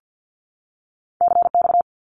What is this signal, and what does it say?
Morse code, heard as a keyed tone, about 50 wpm, 700 Hz. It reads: CX
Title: Abbreviations 50wpm